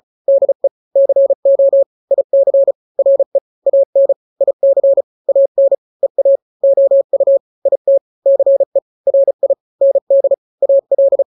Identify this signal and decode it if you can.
Morse code, heard as a keyed tone, about 35 wpm, 550 Hz. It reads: DE CO IC RE AN IC AN EA OU IT CE RI ND AL
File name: n-grams / 2-grams 35 30 WPM